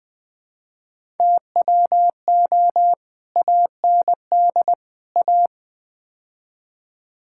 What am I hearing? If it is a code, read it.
Morse code, heard as a keyed tone, about 20 wpm, 700 Hz. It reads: TWO AND A
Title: Build A Phrase - 5-Word Phrases - 20wpm